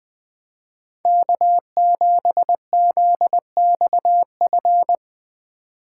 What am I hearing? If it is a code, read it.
Morse code, heard as a keyed tone, about 20 wpm, 700 Hz. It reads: K7ZXF